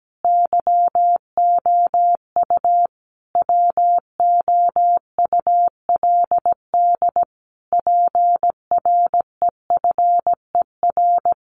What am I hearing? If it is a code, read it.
Morse code, heard as a keyed tone, about 17 wpm, 700 Hz. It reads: YOU WOULD PREFER
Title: Sentences from Top 1100 Words 17wpm